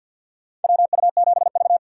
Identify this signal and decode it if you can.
Morse code, heard as a keyed tone, about 53 wpm, 700 Hz. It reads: WU7V
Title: USA Call Signs - 4 Characters - 6x Character Spacing - 50wpm